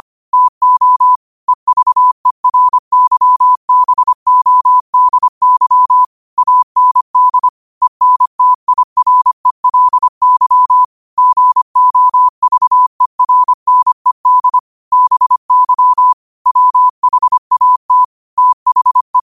Morse code, heard as a keyed tone, about 25 wpm, 1000 Hz. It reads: TO EVERYBODY AND ENTIRELY GOVERNED BY WHAT THE
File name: qotdfiles / 25wpm